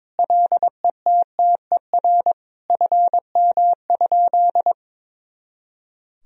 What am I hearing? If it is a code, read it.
Morse code, heard as a keyed tone, about 22 wpm, 700 Hz. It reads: LETTER FM?
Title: Sentences from Top 300 Words and Abbreviations 22wpm